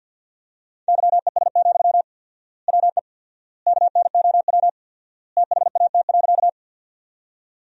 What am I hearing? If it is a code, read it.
Morse code, heard as a keyed tone, about 47 wpm, 700 Hz. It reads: YES, WE KNOW THAT.
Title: 4x Word-Spacing - Sentences from Top 500 Words - 50wpm